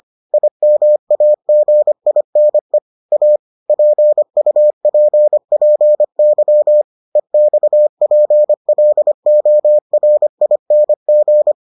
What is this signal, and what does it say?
Morse code, heard as a keyed tone, about 25 wpm, 600 Hz. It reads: IMAGINE A PUPPY EXPLORING